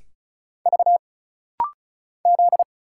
Morse code, heard as a keyed tone, about 34 wpm, 700 Hz. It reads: V Z
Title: Single Letters - Rapid-Fire 35wpm